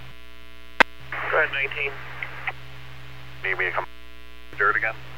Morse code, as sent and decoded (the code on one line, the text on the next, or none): none